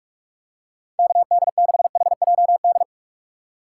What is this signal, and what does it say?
Morse code, heard as a keyed tone, about 44 wpm, 700 Hz. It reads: KD6HJD
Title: USA Call Signs - 6 Characters - 2x Character Spacing - 45wpm